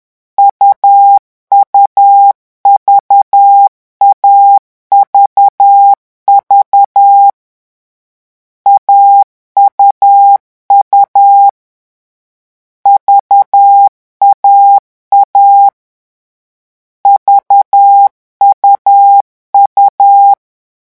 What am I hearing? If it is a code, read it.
Morse code, heard as a keyed tone, about 10 wpm, 800 Hz. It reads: UUV AVV AUU VAA VUU
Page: Leçon 3 - vitesse 10 mots minute